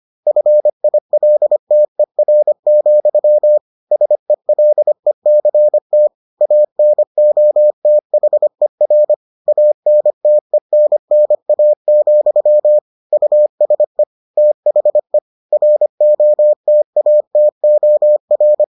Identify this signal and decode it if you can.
Morse code, heard as a keyed tone, about 25 wpm, 600 Hz. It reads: FILTER, SELECT ANOTHER ANTENNA, USE THE ROTATOR